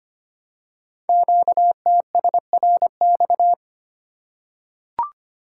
Morse code, heard as a keyed tone, about 25 wpm, 700 Hz. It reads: QTSRX
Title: Code Groups - 5 Letters - 25wpm